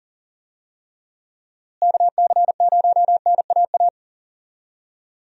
Morse code, heard as a keyed tone, about 40 wpm, 700 Hz. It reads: KC0NAA
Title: USA Call Signs - 6 Characters - Repeated 3x - 40wpm